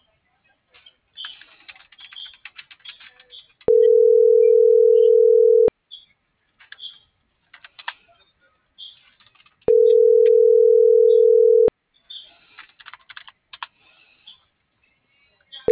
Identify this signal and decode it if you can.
Ringback tone (North America), 440 + 480 Hz.